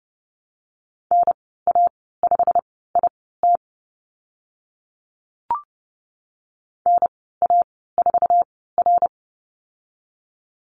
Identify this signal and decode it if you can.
Morse code, heard as a keyed tone, about 30 wpm, spaced as for 17 wpm, 700 Hz. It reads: NA5IT NA4R
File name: Lesson 15g - Number 5 - Call Signs 30wpm